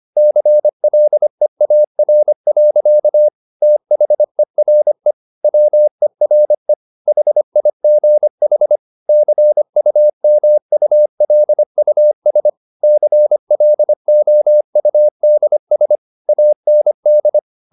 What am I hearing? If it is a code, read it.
Morse code, heard as a keyed tone, about 25 wpm, 600 Hz. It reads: CLEAR. THERE WERE HIGH CUMULUS CLOUDS AND